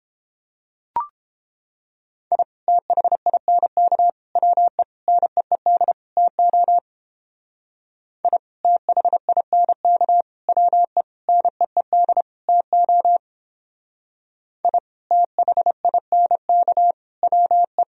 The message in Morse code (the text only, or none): I THINK WE NEED TO I THINK WE NEED TO I THINK WE